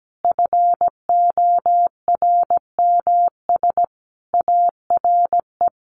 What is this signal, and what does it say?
Morse code, heard as a keyed tone, about 17 wpm, 700 Hz. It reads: FORMS ARE
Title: A Sound Of Thunder 17wpm